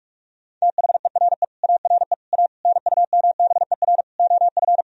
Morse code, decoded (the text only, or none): THERE ARE A NUMBER OF